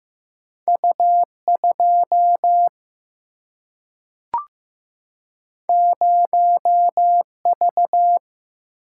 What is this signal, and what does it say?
Morse code, heard as a keyed tone, about 15 wpm, 700 Hz. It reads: U2 0V
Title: Single-Letter Single-Number 15wpm